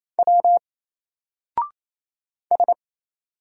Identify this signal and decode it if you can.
Morse code, heard as a keyed tone, about 28 wpm, 700 Hz. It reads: W S
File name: Single-Letter-Number-28wpm